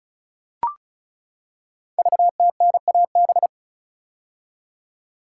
Morse code, heard as a keyed tone, about 36 wpm, 700 Hz. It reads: VTNAB
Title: Code Groups - 5 Letters - 35wpm